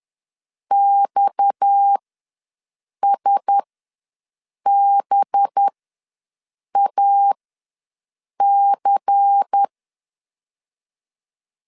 Morse code, heard as a keyed tone, about 10 wpm, spaced as for 6 wpm, 800 Hz. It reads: XSBAC
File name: LEZIONI / 6 PKX0 / 6 PKX0 ws 10wpm 4220